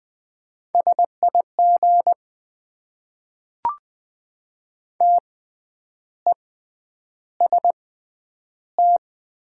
Morse code, heard as a keyed tone, about 20 wpm, spaced as for 9 wpm, 700 Hz. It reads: SIG TEST
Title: Top 90 QSO Elements - 6x Character Spacing - 20wpm